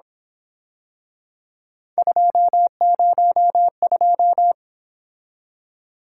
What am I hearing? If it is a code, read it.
Morse code, heard as a keyed tone, about 27 wpm, 700 Hz. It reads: 202
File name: Speed-Racing - Three Letter-Number Combinations - 22wpm